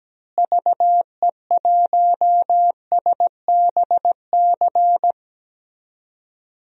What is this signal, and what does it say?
Morse code, heard as a keyed tone, about 17 wpm, 700 Hz. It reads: VE1SBC
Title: DXCC-100 Call Signs - 6 Characters - Repeated 2x - 17wpm